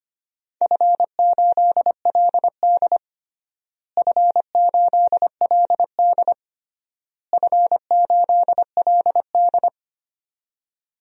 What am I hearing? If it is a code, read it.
Morse code, heard as a keyed tone, about 25 wpm, 700 Hz. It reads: F8LD F8LD F8LD
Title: DXCC-100 Call Signs - 4 Characters - Repeated 3x - 25wpm